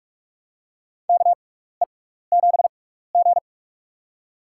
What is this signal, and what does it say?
Morse code, heard as a keyed tone, about 46 wpm, spaced as for 16 wpm, 700 Hz. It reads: KE7G
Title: USA Call Signs - 4 Characters - 6x Character Spacing - 45wpm